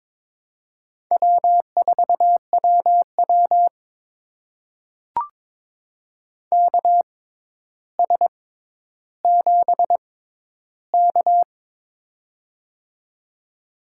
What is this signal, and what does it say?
Morse code, heard as a keyed tone, about 22 wpm, spaced as for 11 wpm, 700 Hz. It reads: W4WW KS7K
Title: USA Call Signs - 4 Characters - 6x Character Spacing - 22wpm